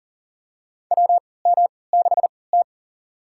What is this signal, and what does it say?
Morse code, heard as a keyed tone, about 40 wpm, spaced as for 23 wpm, 700 Hz. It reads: WM6T
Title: USA Call Signs - 3x Character Spacing - 40wpm